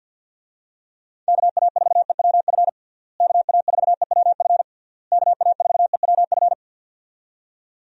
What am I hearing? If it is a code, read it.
Morse code, heard as a keyed tone, about 50 wpm, 700 Hz. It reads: KA4EWF KA4EWF KA4EWF
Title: USA Call Signs - 6 Characters - Repeated 3x - 50wpm